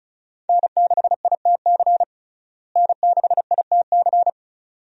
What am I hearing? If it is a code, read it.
Morse code, heard as a keyed tone, about 35 wpm, 700 Hz. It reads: N6ITC N6ITC
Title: USA Call Signs - 5 Characters - Repeated 2x - 35wpm